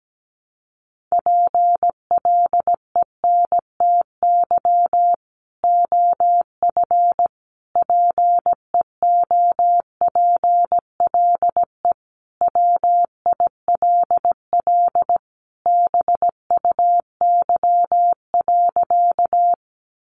Morse code, heard as a keyed tone, about 17 wpm, 700 Hz. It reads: PLENTY OF PEOPLE WILL BUY.
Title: Sentences from Top 1300 Words 17wpm